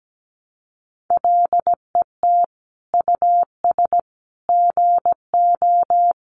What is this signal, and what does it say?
Morse code, heard as a keyed tone, about 17 wpm, 700 Hz. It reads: LET US GO
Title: Sentences from Top 1100 Words 17wpm